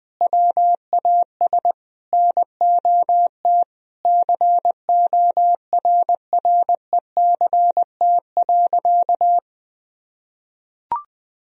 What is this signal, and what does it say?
Morse code, heard as a keyed tone, about 20 wpm, 700 Hz. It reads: WAS NOT CORRECT.
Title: Sentences from Top 1000 Words - Review Part 2 - 20wpm